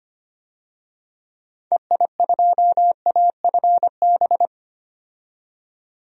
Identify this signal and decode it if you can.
Morse code, heard as a keyed tone, about 25 wpm, 700 Hz. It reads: EI2AFB
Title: DXCC-100 Call Signs - 6 Characters - 25wpm